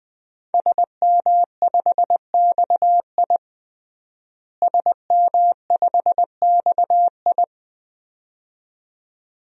Morse code, heard as a keyed tone, about 20 wpm, 700 Hz. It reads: SM5XI SM5XI
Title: DXCC-100 Call Signs - 5 Characters - Repeated 2x - 20wpm